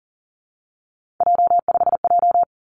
Morse code, heard as a keyed tone, about 42 wpm, 700 Hz. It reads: J5J
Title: Three Challenging Letter-Number Combinations 40wpm